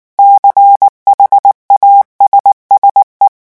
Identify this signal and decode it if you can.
Morse code, heard as a keyed tone, about 19 wpm, 800 Hz. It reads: CHASSE
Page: REVISION 8 - 8 mots minute